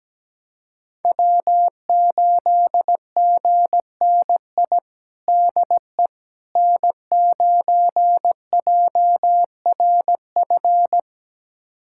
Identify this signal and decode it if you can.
Morse code, heard as a keyed tone, about 17 wpm, 700 Hz. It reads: W8GNI DE N9JRF
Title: USA Callsign DE Callsign - 5 Characters - 17wpm